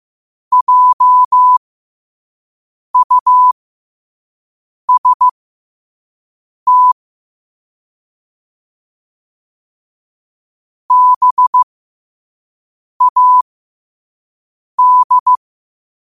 Morse code, heard as a keyed tone, about 15 wpm, spaced as for 5 wpm, 1000 Hz. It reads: JUST BAD
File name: 2024-10-07-5wpm